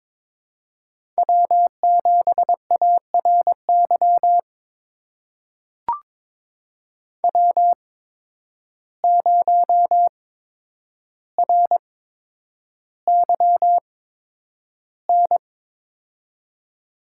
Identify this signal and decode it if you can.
Morse code, heard as a keyed tone, about 22 wpm, spaced as for 9 wpm, 700 Hz. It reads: W7ARY W0RYN
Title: USA Call Signs - 5 Characters - 8x Character Spacing - 22wpm